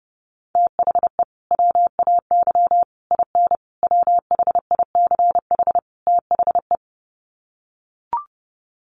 Morse code, heard as a keyed tone, about 30 wpm, 700 Hz. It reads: THE WAY IN WHICH THE
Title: Sets of 5 Words Encore2 30wpm